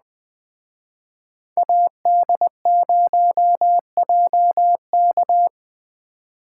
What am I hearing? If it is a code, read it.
Morse code, heard as a keyed tone, about 20 wpm, 700 Hz. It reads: AD0JK
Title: Speed Racing - USA 2x2 Call Signs 17wpm